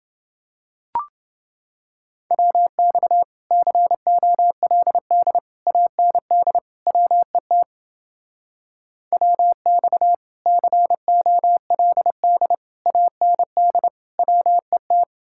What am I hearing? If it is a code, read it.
Morse code, heard as a keyed tone, about 29 wpm, 700 Hz. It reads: WX COLD AND WET WX COLD AND WET